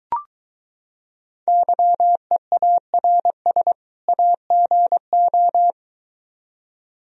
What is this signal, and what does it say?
Morse code, heard as a keyed tone, about 23 wpm, 700 Hz. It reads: YEARS AGO